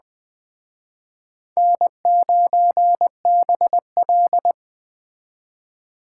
Morse code, heard as a keyed tone, about 21 wpm, 700 Hz. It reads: N9BL